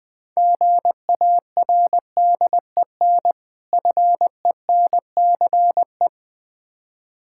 Morse code, heard as a keyed tone, about 20 wpm, 700 Hz. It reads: GARDEN FENCE
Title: Difficult Words - Speed-Racing with Hints - 20wpm